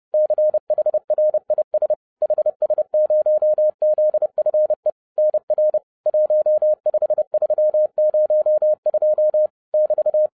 Morse code, heard as a keyed tone, about 30 wpm, 610 Hz. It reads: CHRIS HS0ZFE NR 15302 <BT>